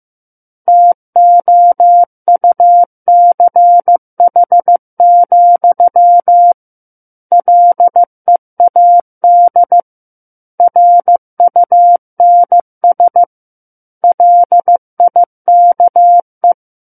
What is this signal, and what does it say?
Morse code, heard as a keyed tone, about 15 wpm, 700 Hz. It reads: TOUCH, LEAD RUNS LIKE
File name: Chapter 6 - 15 WPM